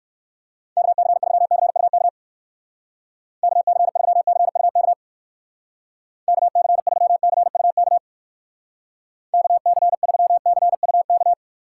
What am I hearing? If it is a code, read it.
Morse code, heard as a keyed tone, about 58 wpm, 700 Hz. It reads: KC3CUK KC3CUK KC3CUK KC3CUK